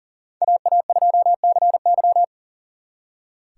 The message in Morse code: AA2CY

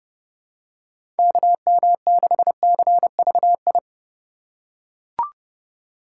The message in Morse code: KM6CVI

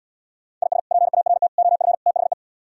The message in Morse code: IS PART OF THE

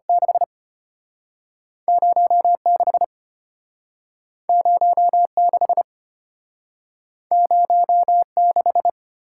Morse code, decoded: 6 06 06 06

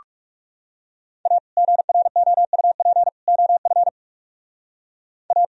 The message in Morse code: A GROUP OF A

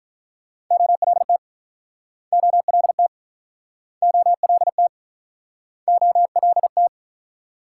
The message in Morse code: OLT OLT OLT OLT

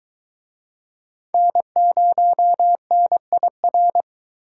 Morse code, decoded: N0NIR

